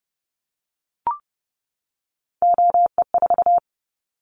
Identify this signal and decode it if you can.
Morse code, heard as a keyed tone, about 30 wpm, 700 Hz. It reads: OE4
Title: Callsign Prefixes for Top 100 DXCC - 30wpm